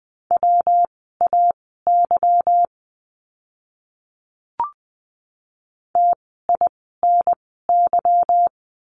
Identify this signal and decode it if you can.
Morse code, heard as a keyed tone, about 20 wpm, spaced as for 15 wpm, 700 Hz. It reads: WAY TINY
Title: 146 Lesson 24f - Letter y - Words 20wpm